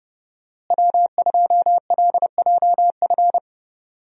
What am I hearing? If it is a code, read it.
Morse code, heard as a keyed tone, about 30 wpm, 700 Hz. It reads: W2LJF
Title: USA Call Signs - 5 Characters - Repeated 2x - 30wpm